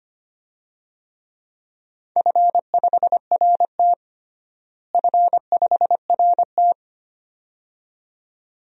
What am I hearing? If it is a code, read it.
Morse code, heard as a keyed tone, about 25 wpm, 700 Hz. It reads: F5RT F5RT
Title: DXCC-100 Call Signs - 4 Characters - Repeated 2x - 25wpm